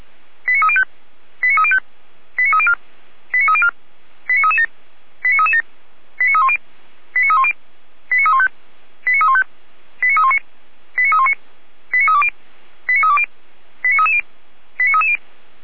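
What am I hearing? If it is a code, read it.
Five-tone selective call (ZVEI), heard as quick runs of five tones, tones of about 80 ms. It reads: 89305 89305 89304 89304 89337 89337 89310 89310 89315 89315 89319 89319 89320 89320 89330 89330
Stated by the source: ZVEI five-tone selective call received on SDR, from the feuerwehr-orlamuende.org fire brigade server.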